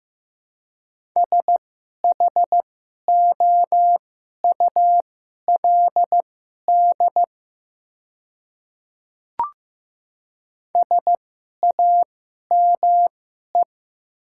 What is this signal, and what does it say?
Morse code, heard as a keyed tone, about 15 wpm, spaced as for 11 wpm, 700 Hz. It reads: SHOULD SAME
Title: 2x Character Spacing - Top 300 Words - 15wpm